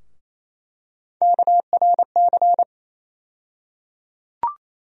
Morse code, heard as a keyed tone, about 27 wpm, 700 Hz. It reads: KRC
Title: USA Call Sign Postfixes - 3 Letters 28wpm